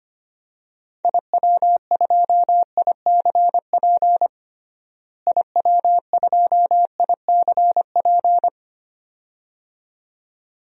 Morse code, heard as a keyed tone, about 25 wpm, 700 Hz. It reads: IW2ICP IW2ICP
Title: DXCC-100 Call Signs - 6 Characters - Repeated 2x - 25wpm